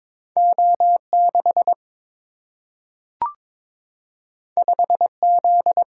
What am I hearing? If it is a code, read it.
Morse code, heard as a keyed tone, about 22 wpm, 700 Hz. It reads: O 6 5 Z
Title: Single-Letter Single-Number 22wpm